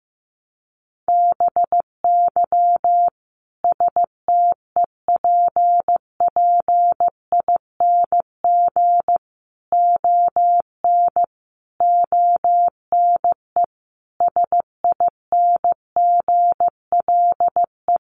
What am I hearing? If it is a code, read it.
Morse code, heard as a keyed tone, about 15 wpm, 700 Hz. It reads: BY STEPPING ON ONE SINGLE
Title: A Sound Of Thunder 15wpm